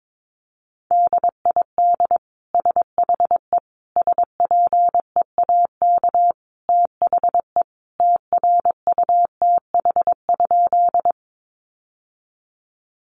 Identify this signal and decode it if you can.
Morse code, heard as a keyed tone, about 22 wpm, 700 Hz. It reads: DID SHE SPEAK THE TRUTH?
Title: Sentences from Top 1100 Words 22wpm